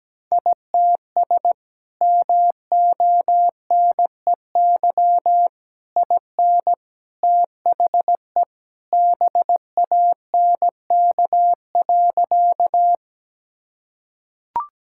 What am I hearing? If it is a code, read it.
Morse code, heard as a keyed tone, about 17 wpm, 700 Hz. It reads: ITS MONEY IN THE BANK.
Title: Sentences from Top 1000 Words - Review Part 6 - 17wpm